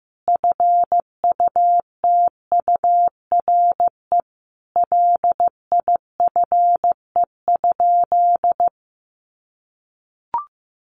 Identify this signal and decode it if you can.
Morse code, heard as a keyed tone, about 15 wpm, 700 Hz. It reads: FUTURE LIFE?
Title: Sentences from Top 1000 Words 15wpm